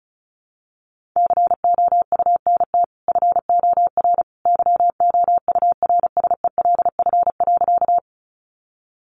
Morse code, heard as a keyed tone, about 34 wpm, 700 Hz. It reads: COUNT FOR YOURSELF.
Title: Sentences from Top 1500 Words 35wpm